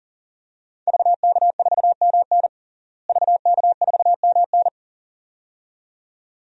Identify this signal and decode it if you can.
Morse code, heard as a keyed tone, about 40 wpm, 700 Hz. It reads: VK4MN VK4MN
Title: DXCC-100 Call Signs - 5 Characters - Repeated 2x - 40wpm